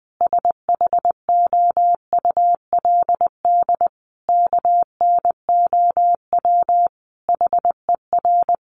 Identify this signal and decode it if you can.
Morse code, heard as a keyed tone, about 20 wpm, 700 Hz. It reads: SHOULD KNOW HER
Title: Sentences from Top 400 Words 20wpm